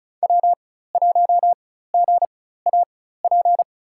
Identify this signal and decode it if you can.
Morse code, heard as a keyed tone, about 36 wpm, spaced as for 16 wpm, 700 Hz. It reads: W1GAP
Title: USA Call Signs - 5 Characters - 4x Character Spacing - 35wpm